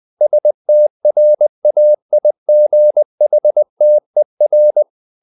Morse code, heard as a keyed tone, about 20 wpm, 600 Hz. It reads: STRAIGHTER
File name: the old man and the sea / the old man and the sea part 1 20wpm